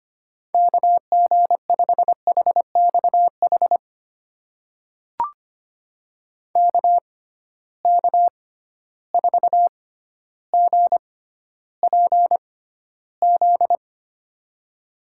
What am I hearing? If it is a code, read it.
Morse code, heard as a keyed tone, about 25 wpm, spaced as for 13 wpm, 700 Hz. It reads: KG5HXH KK4GPZ
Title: USA Call Signs - 6x Character Spacing - 25wpm